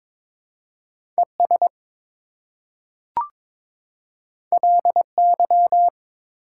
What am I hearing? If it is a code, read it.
Morse code, heard as a keyed tone, about 23 wpm, 700 Hz. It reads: ES LY
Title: Prefixes and Suffixes 22wpm